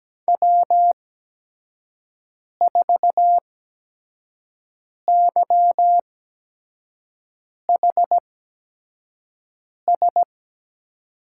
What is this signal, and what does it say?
Morse code, heard as a keyed tone, about 17 wpm, spaced as for 5 wpm, 700 Hz. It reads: W4YHS
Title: USA Call Signs - 5 Characters - 8x Character Spacing - 17wpm